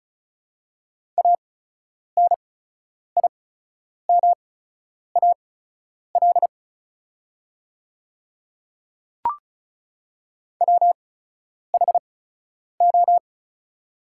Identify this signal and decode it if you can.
Morse code, heard as a keyed tone, about 36 wpm, spaced as for 10 wpm, 690 Hz. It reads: ANIMAL WHO
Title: 8x Character Spacing - Top 300 Words - 35wpm